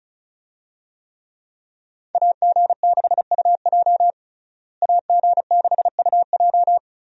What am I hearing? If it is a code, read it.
Morse code, heard as a keyed tone, about 34 wpm, 700 Hz. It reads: AG6UJ AG6UJ
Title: USA Call Signs - 5 Characters - Repeated 2x - 35wpm